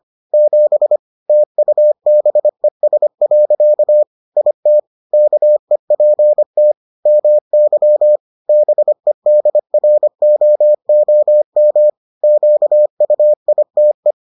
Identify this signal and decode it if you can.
Morse code, heard as a keyed tone, about 25 wpm, 600 Hz. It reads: 7 TUBES. IT KEPT MY BEDROOM QUITE